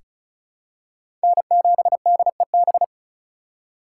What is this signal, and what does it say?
Morse code, heard as a keyed tone, about 36 wpm, 700 Hz. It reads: N7DEB
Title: USA Call Signs - 5 Characters - 4x Character Spacing - 35wpm